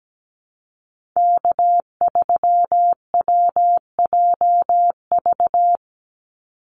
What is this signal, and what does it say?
Morse code, heard as a keyed tone, about 17 wpm, 700 Hz. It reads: K3WJV
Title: Call Signs Repeated 3x - 17wpm